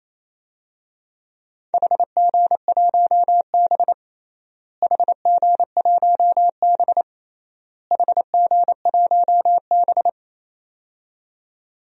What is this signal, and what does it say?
Morse code, heard as a keyed tone, about 28 wpm, 700 Hz. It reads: HG1B HG1B HG1B
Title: DXCC-100 Call Signs - 4 Characters - Repeated 3x - 28wpm